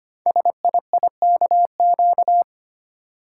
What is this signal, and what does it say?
Morse code, heard as a keyed tone, about 25 wpm, 700 Hz. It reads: SIIKQ
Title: Code Groups - 5 Letters - 25wpm